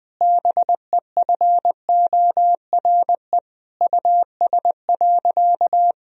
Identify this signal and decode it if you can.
Morse code, heard as a keyed tone, about 20 wpm, 700 Hz. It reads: BEFORE US.
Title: Sentences from Top 1000 Words - Review Part 4 - 20wpm